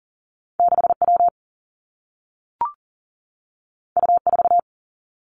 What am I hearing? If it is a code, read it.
Morse code, heard as a keyed tone, about 43 wpm, 700 Hz. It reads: 6 W U 4
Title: Two Challenging Letter-Number Combinations 40wpm